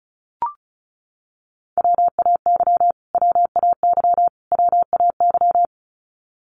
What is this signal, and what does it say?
Morse code, heard as a keyed tone, about 34 wpm, 700 Hz. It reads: WAY WAY WAY
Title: Top 300 Words Repeated 35wpm